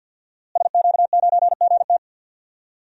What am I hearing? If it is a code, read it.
Morse code, heard as a keyed tone, about 46 wpm, 700 Hz. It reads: SQ9GT
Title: DXCC-100 Call Signs - 5 Characters - Repeated 3x - 50wpm